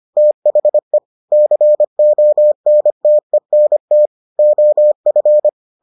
Morse code, heard as a keyed tone, about 25 wpm, 600 Hz. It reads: THE CONTENT OF